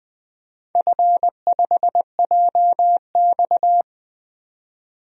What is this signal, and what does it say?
Morse code, heard as a keyed tone, about 21 wpm, 700 Hz. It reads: F5JX
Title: DXCC-100 Call Signs - 4 Characters - 20wpm